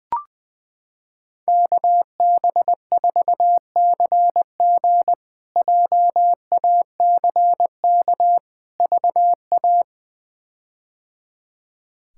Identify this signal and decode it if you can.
Morse code, heard as a keyed tone, about 20 wpm, 700 Hz. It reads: KB4CG JACK VA
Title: K1USN SST QSO Practice - 20wpm